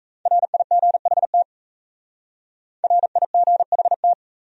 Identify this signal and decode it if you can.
Morse code, heard as a keyed tone, about 34 wpm, 700 Hz. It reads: RIGHT RIGHT
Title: Speed-Racing Top 200 Words 28wpm